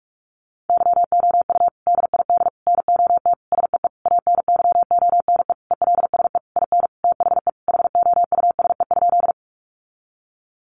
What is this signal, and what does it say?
Morse code, heard as a keyed tone, about 44 wpm, 700 Hz. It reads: YOU DID NOT SEE ANYONE ELSE IN THE HOUSE?
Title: Sentences from Top 900 Words 45wpm